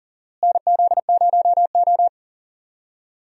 Morse code, heard as a keyed tone, about 40 wpm, 700 Hz. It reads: NZ0O
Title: USA Call Signs - 4 Characters - Repeated 2x - 40wpm